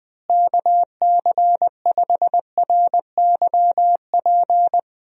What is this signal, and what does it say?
Morse code, heard as a keyed tone, about 21 wpm, 700 Hz. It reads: KC5RYP
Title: USA Call Signs - 6 Characters - Repeated 2x - 20wpm